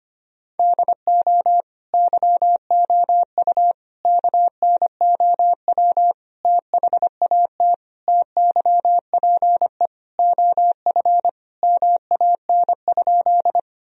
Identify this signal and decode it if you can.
Morse code, heard as a keyed tone, about 25 wpm, 700 Hz. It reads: DO YOU KNOW THAT TYPE OF MAN?
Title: Sentences from Top 1000 Words - Review Part 3 - 25wpm